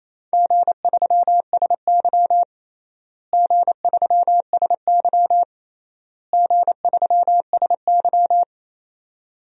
Morse code, heard as a keyed tone, about 28 wpm, 700 Hz. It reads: G3SY G3SY G3SY
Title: DXCC-100 Call Signs - 4 Characters - Repeated 3x - 28wpm